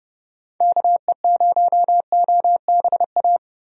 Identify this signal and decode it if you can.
Morse code, heard as a keyed tone, about 32 wpm, 700 Hz. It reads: KE0OBA
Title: USA Call Signs - 6 Characters - Repeated 2x - 30wpm